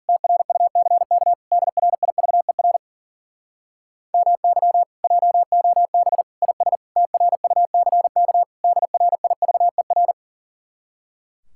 Morse code, heard as a keyed tone, about 52 wpm, 700 Hz. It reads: TRUCK DRIVER MY JOB IS TRUCK DRIVER